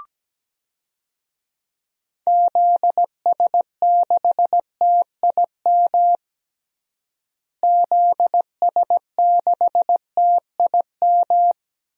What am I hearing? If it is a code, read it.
Morse code, heard as a keyed tone, about 17 wpm, 700 Hz. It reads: ZS6TIM ZS6TIM
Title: DXCC-100 Call Signs - 6 Characters - Repeated 2x - 17wpm